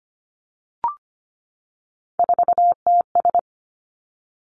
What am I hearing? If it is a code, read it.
Morse code, heard as a keyed tone, about 25 wpm, 700 Hz. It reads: 4TS
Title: Three Challenging Letter-Number Combinations 25wpm